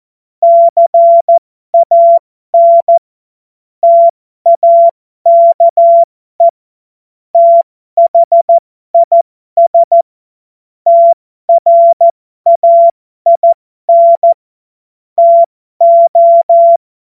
Morse code, heard as a keyed tone, about 14 wpm, spaced as for 12 wpm, 680 Hz. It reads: CAN TAKE THIS TRAIN TO
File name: Morsekurs / Witze